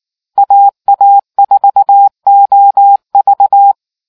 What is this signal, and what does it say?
Morse code, heard as a keyed tone, about 19 wpm, 800 Hz. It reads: AA4OV